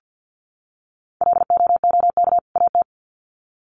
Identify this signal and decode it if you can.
Morse code, heard as a keyed tone, about 50 wpm, 700 Hz. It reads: LOOK AT
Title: Sets of 2 Words 50wpm